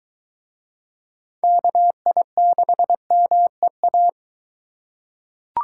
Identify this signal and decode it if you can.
Morse code, heard as a keyed tone, about 22 wpm, 700 Hz. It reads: KI6MEA